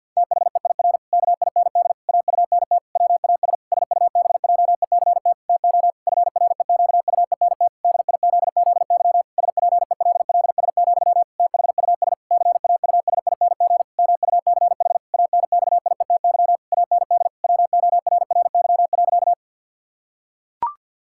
Morse code, heard as a keyed tone, about 50 wpm, 700 Hz. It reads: THEIR KIND AUNT WAS SUBJECT TO FREQUENT DIZZY SPELLS, THUS CAUSING MUCH ANXIETY AND WORRY.
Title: Pangram Challenge v2 - 50wpm